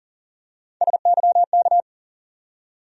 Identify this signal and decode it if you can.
Morse code, heard as a keyed tone, about 40 wpm, 700 Hz. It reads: SYK
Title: USA Call Sign Postfixes - 3 Letters 40wpm